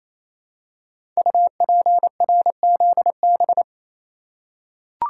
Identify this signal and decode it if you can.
Morse code, heard as a keyed tone, about 27 wpm, 700 Hz. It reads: UPRZB
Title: Code Groups - 5 Letters - 28wpm